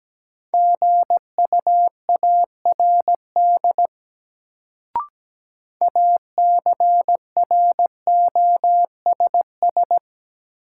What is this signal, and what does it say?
Morse code, heard as a keyed tone, about 17 wpm, 700 Hz. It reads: GUARD ACROSS
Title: Sentences from Top 700 Words - New Word Drill - 17wpm